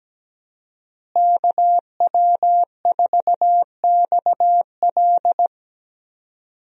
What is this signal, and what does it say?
Morse code, heard as a keyed tone, about 17 wpm, 700 Hz. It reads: KW4XL
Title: USA Call Signs - 5 Characters - Repeated 3x - 17wpm